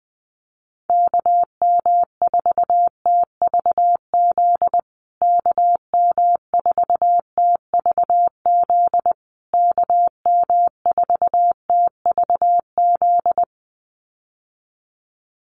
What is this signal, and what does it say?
Morse code, heard as a keyed tone, about 20 wpm, 700 Hz. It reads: KM4TVZ KM4TVZ KM4TVZ
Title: USA Call Signs Repeated 3x 20wpm